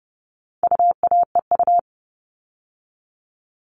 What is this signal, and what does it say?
Morse code, heard as a keyed tone, about 30 wpm, 700 Hz. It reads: UAEU
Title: All Cut Numbers - 4-Digit 30wpm